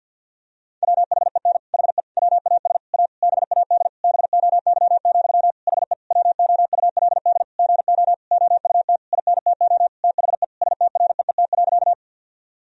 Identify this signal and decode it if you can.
Morse code, heard as a keyed tone, about 52 wpm, 700 Hz. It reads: WHEN HE WAS A BAD BOY, HE WOULD GO OUT INTO THE STREET.
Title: Sentences from Top 1000 Words - Review Part 3 - 50wpm